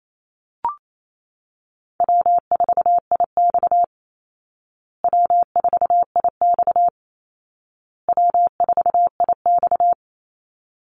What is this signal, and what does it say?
Morse code, heard as a keyed tone, about 28 wpm, 700 Hz. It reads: W4IX W4IX W4IX
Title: Call Signs Repeated 2x - 28wpm